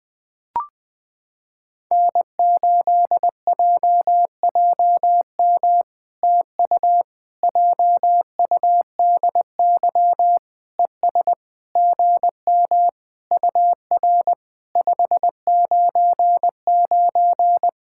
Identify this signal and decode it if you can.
Morse code, heard as a keyed tone, about 20 wpm, 700 Hz. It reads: N8JJM TU JUDY ES GM UR 599
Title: POTA QSO Practice - 20wpm